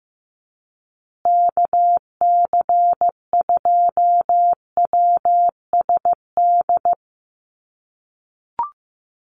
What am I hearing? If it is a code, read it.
Morse code, heard as a keyed tone, about 15 wpm, 700 Hz. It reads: KC2WSD
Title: USA Call Signs Repeated 3x 15wpm